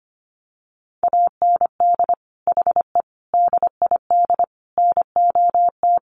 Morse code, heard as a keyed tone, about 25 wpm, 700 Hz. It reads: AND HE DID NOT
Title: Sentences from Top 1500 Words 25wpm